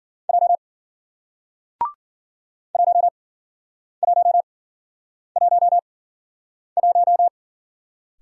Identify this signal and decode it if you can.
Morse code, heard as a keyed tone, about 53 wpm, spaced as for 9 wpm, 700 Hz. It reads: J 1111